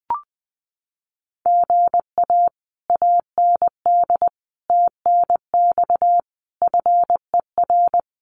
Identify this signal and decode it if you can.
Morse code, heard as a keyed tone, about 19 wpm, 700 Hz. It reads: GA AND TNX FER
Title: Common QSO Phrases Encore 20wpm